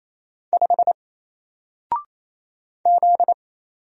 Morse code, heard as a keyed tone, about 29 wpm, 700 Hz. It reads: 5 Z
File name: Single-Letter-Number-28wpm